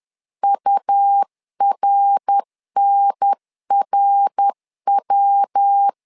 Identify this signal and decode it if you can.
Morse code, heard as a keyed tone, about 10 wpm, 800 Hz. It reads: URNRW
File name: LEZIONI / 3 MWRO / 3 MWR 10wpm 1952